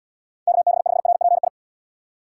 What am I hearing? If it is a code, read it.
Morse code, heard as a keyed tone, about 72 wpm, 700 Hz. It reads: KB5RPI